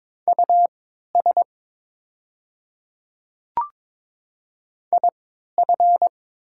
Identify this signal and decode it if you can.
Morse code, heard as a keyed tone, about 22 wpm, spaced as for 12 wpm, 700 Hz. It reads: US IF
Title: 3x Character Spacing - Top 200 Words - 22wpm